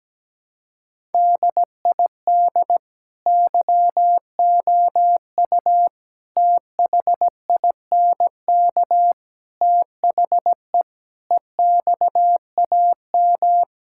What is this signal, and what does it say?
Morse code, heard as a keyed tone, about 17 wpm, 700 Hz. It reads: DID YOU THINK THE EXAM
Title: Sentences from Top 1000 Words - Review Part 5 - 17wpm